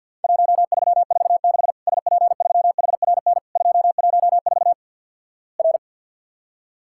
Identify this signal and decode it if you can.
Morse code, heard as a keyed tone, about 50 wpm, 700 Hz. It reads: 1346 SP3HRN 214 R
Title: Worked All Europe DX Contest - QTC Exchange - 50wpm